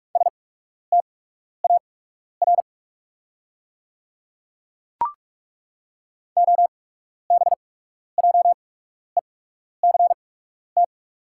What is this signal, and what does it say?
Morse code, heard as a keyed tone, about 43 wpm, spaced as for 12 wpm, 690 Hz. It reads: STAR OBJECT
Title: 8x Character Spacing - Top 500 Words - 45wpm